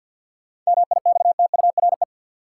Morse code, heard as a keyed tone, about 53 wpm, 700 Hz. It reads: MIXTURE
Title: Sentences from Top 1500 Words - New Word Drill - 50wpm